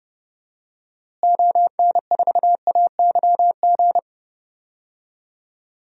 Morse code, heard as a keyed tone, about 30 wpm, 700 Hz. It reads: ON4AYG
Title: DXCC-100 Call Signs - 6 Characters - Repeated 2x - 30wpm